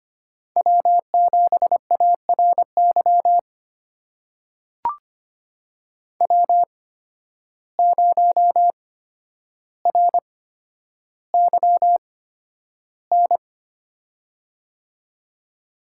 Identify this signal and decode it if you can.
Morse code, heard as a keyed tone, about 25 wpm, spaced as for 11 wpm, 700 Hz. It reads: W7ARY W0RYN
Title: USA Call Signs - 5 Characters - 8x Character Spacing - 25wpm